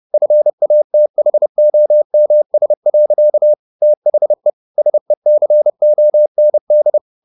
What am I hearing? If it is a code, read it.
Morse code, heard as a keyed tone, about 30 wpm, 600 Hz. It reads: FATHOMS. THE SECOND
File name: the old man and the sea / the old man and the sea 1 30wpm